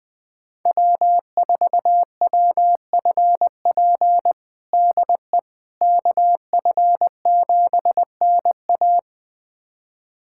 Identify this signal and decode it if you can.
Morse code, heard as a keyed tone, about 20 wpm, 700 Hz. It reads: W4WFP DE KF7NA
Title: USA Callsign DE Callsign - 5 Characters - 20wpm